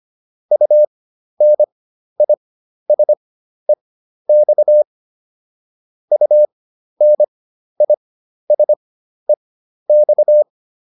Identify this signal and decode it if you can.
Morse code, heard as a keyed tone, about 25 wpm, spaced as for 12 wpm, 600 Hz. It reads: UNISEX UNISEX